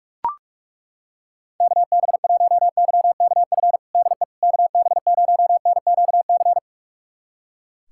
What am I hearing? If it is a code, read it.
Morse code, heard as a keyed tone, about 41 wpm, 700 Hz. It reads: KB1YKF DE KB0NQC
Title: USA Callsign DE Callsign - 6 Characters - 45wpm